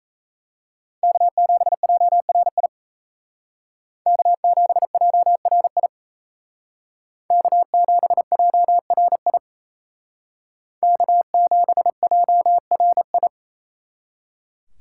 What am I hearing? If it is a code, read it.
Morse code, heard as a keyed tone, about 35 wpm, 700 Hz. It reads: K7JRI K7JRI K7JRI K7JRI